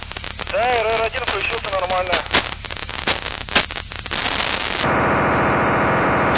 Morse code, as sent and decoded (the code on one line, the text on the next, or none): none